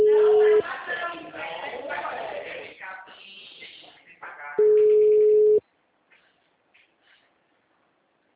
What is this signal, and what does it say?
Ringback tone (Europe, 425 Hz), with speech.